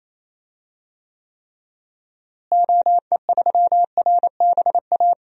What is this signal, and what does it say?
Morse code, heard as a keyed tone, about 28 wpm, 700 Hz. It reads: OE3RBA
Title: DXCC-100 Call Signs - 6 Characters - Repeated 3x - 28wpm